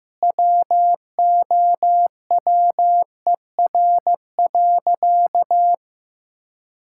Morse code, heard as a keyed tone, about 15 wpm, 700 Hz. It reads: WOWER.
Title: Pangram Challenge v2 - 15wpm